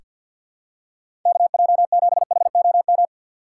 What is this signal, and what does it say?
Morse code, heard as a keyed tone, about 47 wpm, 700 Hz. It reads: KJ7HOM
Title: USA Call Signs - 6x Character Spacing - 50wpm